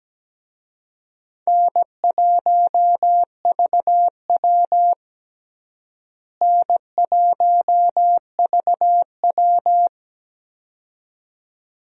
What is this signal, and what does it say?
Morse code, heard as a keyed tone, about 17 wpm, 700 Hz. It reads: N1VW N1VW
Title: USA Call Signs - 4 Characters - Repeated 2x - 17wpm